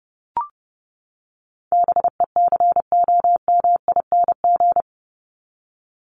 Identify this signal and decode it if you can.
Morse code, heard as a keyed tone, about 30 wpm, 700 Hz. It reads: BECOMING
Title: ING Suffix - Top 100 Words with Base Words 30wpm